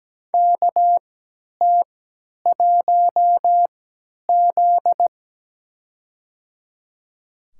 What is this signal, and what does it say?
Morse code, heard as a keyed tone, about 17 wpm, spaced as for 10 wpm, 700 Hz. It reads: KT1Z
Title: USA Call Signs - 4 Characters - 3x Character Spacing - 17wpm